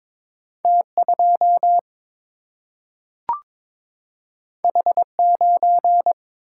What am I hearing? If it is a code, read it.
Morse code, heard as a keyed tone, about 22 wpm, 700 Hz. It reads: T 2 H 9
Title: Single-Letter Single-Number 22wpm